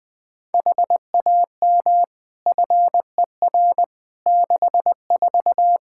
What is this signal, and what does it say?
Morse code, heard as a keyed tone, about 20 wpm, 700 Hz. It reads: HAM FER 64
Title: Practice QSO - Exchange 3 - More Info - 20wpm